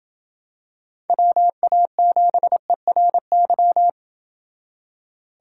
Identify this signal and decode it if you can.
Morse code, heard as a keyed tone, about 26 wpm, 700 Hz. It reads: WA7ERY